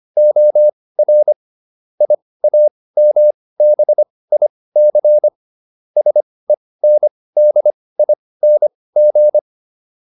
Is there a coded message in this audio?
Yes: Morse code from 0.2 to 9.4 s, heard as a keyed tone, about 25 wpm, spaced as for 18 wpm, 600 Hz. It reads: OR IAMBIC SENDING